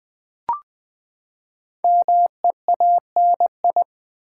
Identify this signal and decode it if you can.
Morse code, heard as a keyed tone, about 20 wpm, 700 Hz. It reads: MEANI
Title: Build A Word - 5 Letters At A Time - 10-Letter Words - 20wpm